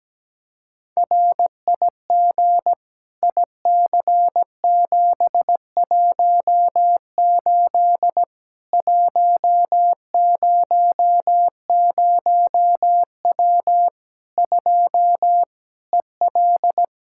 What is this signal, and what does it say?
Morse code, heard as a keyed tone, about 17 wpm, 700 Hz. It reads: RIG IC718 100W 2 EL
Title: Practice QSO - Exchange 2 - Info - 17wpm